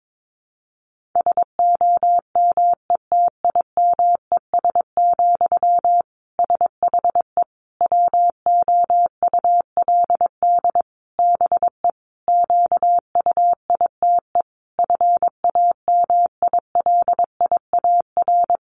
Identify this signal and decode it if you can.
Morse code, heard as a keyed tone, about 22 wpm, 700 Hz. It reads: SOMETIMES, SHE WOULD BE QUITE FAMILIAR
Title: Sentences from Top 1000 Words 22wpm